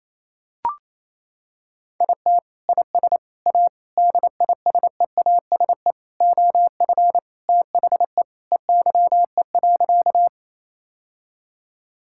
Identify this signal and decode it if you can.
Morse code, heard as a keyed tone, about 28 wpm, 700 Hz. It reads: IT IS A DISEASE OF THE EYE.
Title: Sentences from Top 1000 Words - Review Part 6 - 28wpm